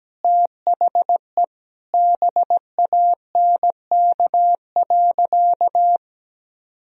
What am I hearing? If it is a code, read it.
Morse code, heard as a keyed tone, about 17 wpm, 700 Hz. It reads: THE BANK.
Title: Sentences from Top 1000 Words - Review Part 4 - 17wpm